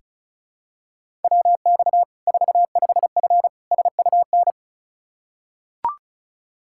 Morse code, heard as a keyed tone, about 34 wpm, 700 Hz. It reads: WX 45F SUN
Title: Practice QSO - Exchange 2 - Info - 35wpm